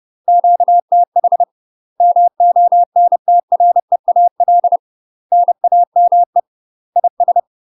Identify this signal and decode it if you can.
Morse code, heard as a keyed tone, about 30 wpm, 700 Hz. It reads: QTH MONTREAL NAME IS
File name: UK MASTER QSOs TRAINING/30WPM